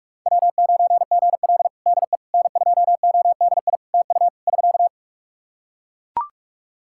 Morse code, heard as a keyed tone, about 41 wpm, 700 Hz. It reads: W9GL DE N2ODI TU <SK>
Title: Practice QSO - Exchange 4 - Ending - 45wpm